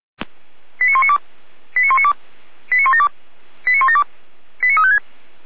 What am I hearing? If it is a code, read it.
Five-tone selective call (ZVEI), heard as quick runs of five tones, tones of about 70 ms. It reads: 89192 89192 89172 89172 89356
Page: SDR/ZVEI